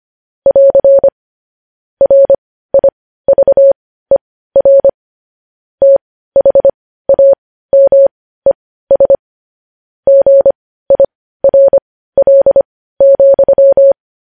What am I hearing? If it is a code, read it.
Morse code, heard as a keyed tone, about 25 wpm, spaced as for 15 wpm, 550 Hz. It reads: <AR> RIVER THAMES GIRL,
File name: podcast / News headlines-1x-25x15